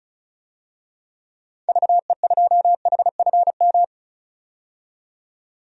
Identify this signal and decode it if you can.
Morse code, heard as a keyed tone, about 36 wpm, 700 Hz. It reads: VE2HFM
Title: DXCC-100 Call Signs - 6 Characters - 35wpm